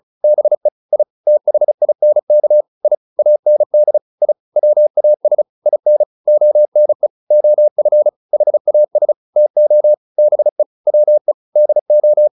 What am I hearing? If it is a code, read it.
Morse code, heard as a keyed tone, about 34 wpm, 600 Hz. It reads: BE I THINK I AND I WAS IN ONE OF HAS TO BE WE DO
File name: WD202 35